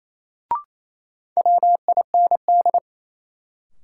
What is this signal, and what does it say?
Morse code, heard as a keyed tone, about 28 wpm, 700 Hz. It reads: WIND
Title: Sentences from Top 400 Words - New Word Drill - 28wpm